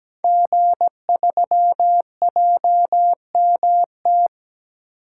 Morse code, heard as a keyed tone, about 17 wpm, 700 Hz. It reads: G3JMT
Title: Code Groups - 5 Letter:Numbers - 17wpm